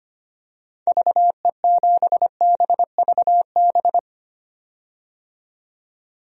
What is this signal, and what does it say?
Morse code, heard as a keyed tone, about 25 wpm, 700 Hz. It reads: VE7BVB
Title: DXCC-100 Call Signs - 6 Characters - 25wpm